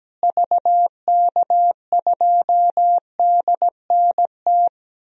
Morse code, heard as a keyed tone, about 17 wpm, 700 Hz. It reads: VK2DNT
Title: DXCC-100 Call Signs - 6 Characters - Repeated 3x - 17wpm